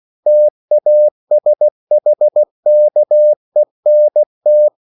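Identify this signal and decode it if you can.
Morse code, heard as a keyed tone, about 16 wpm, 600 Hz. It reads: TASHKENT
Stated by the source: CW practice of national capitals at 16 words per minute, from ON6ZQ.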